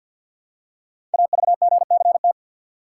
Morse code, heard as a keyed tone, about 52 wpm, 700 Hz. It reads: AVGCT
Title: Code Groups - 5 Letter:Numbers - 50wpm